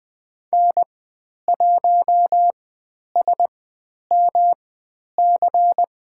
Morse code, heard as a keyed tone, about 20 wpm, spaced as for 10 wpm, 700 Hz. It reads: N1SMC
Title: 109 Lesson 18g - Letter m - Call Signs 20wpm